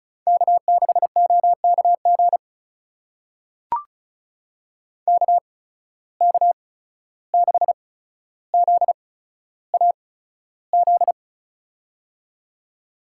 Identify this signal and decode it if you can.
Morse code, heard as a keyed tone, about 36 wpm, spaced as for 14 wpm, 700 Hz. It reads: K6OKG KK6ZAZ
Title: USA Call Signs - 8x Character Spacing - 35wpm